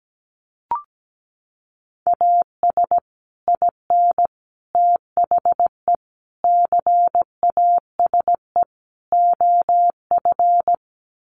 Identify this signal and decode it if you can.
Morse code, heard as a keyed tone, about 17 wpm, 700 Hz. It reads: AS IN THE CASE OF
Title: Sets of 5 Words 17wpm